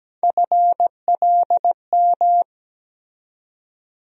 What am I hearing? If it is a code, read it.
Morse code, heard as a keyed tone, about 17 wpm, 700 Hz. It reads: FLM
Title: USA Call Sign Postfixes - 3 Letters 17wpm